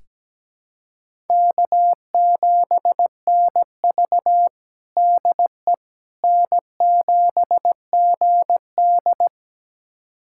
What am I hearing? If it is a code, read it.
Morse code, heard as a keyed tone, about 17 wpm, 700 Hz. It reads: K7NV DE N7GD
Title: USA Callsign DE Callsign - 4 Characters - 17wpm